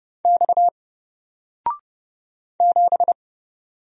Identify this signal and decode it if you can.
Morse code, heard as a keyed tone, about 30 wpm, 700 Hz. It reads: X 7
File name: Single-Letter-Number-30wpm